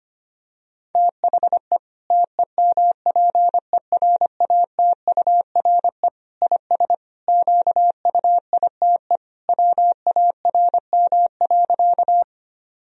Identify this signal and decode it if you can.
Morse code, heard as a keyed tone, about 25 wpm, 700 Hz. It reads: THE TEMPERATURE IS QUITE WARM.
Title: Sentences from Top 1000 Words - Review Part 4 - 25wpm